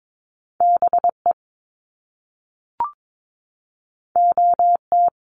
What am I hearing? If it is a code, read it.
Morse code, heard as a keyed tone, about 22 wpm, 700 Hz. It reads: BE OT
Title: Top Two Letter Combinations in Words 22wpm